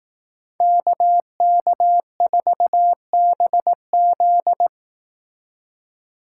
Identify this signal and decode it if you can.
Morse code, heard as a keyed tone, about 18 wpm, 700 Hz. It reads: KK4BZ
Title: Speed Racing - USA 2x2 Call Signs 15wpm